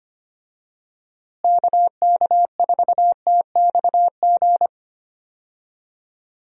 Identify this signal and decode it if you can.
Morse code, heard as a keyed tone, about 25 wpm, 700 Hz. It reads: KK4TXG
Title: USA Call Signs - 6 Characters - Repeated 2x - 25wpm